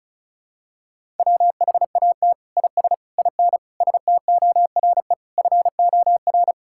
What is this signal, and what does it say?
Morse code, heard as a keyed tone, about 36 wpm, 700 Hz. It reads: WHAT IS IN STORE FOR
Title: Sentences from Top 1000 Words - Review Part 5 - 35wpm